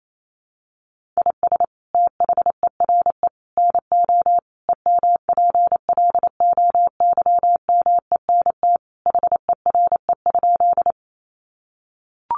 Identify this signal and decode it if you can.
Morse code, heard as a keyed tone, about 28 wpm, 700 Hz. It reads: IS THERE NO EMPLOYMENT HERE?
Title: Sentences from Top 1500 Words 28wpm